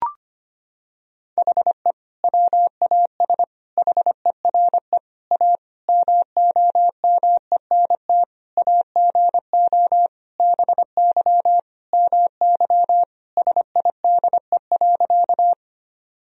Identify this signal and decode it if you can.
Morse code, heard as a keyed tone, about 25 wpm, 700 Hz. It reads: HE WAS HERE A MOMENT AGO BY MY SIDE.
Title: Sentences from Top 1000 Words - Review Part 3 - 25wpm